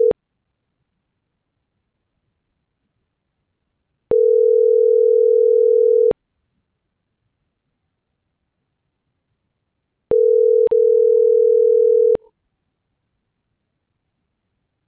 Ringback tone (North America), 440 + 480 Hz.